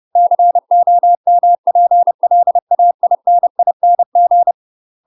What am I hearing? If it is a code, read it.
Morse code, heard as a keyed tone, about 30 wpm, 700 Hz. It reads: COMPLAINING
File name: UK MASTER QSOs TRAINING / 30WPM / UK MASTER QSOs 0030